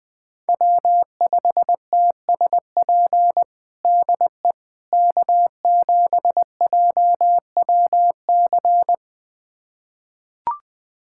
Morse code, heard as a keyed tone, about 20 wpm, 700 Hz. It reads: W5TSP DE K7JWC
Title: USA Callsign DE Callsign - 5 Characters - 20wpm